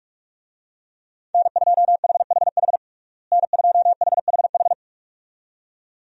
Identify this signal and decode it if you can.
Morse code, heard as a keyed tone, about 46 wpm, 700 Hz. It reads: N2HHH N2HHH
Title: USA Call Signs - 5 Characters - Repeated 2x - 45wpm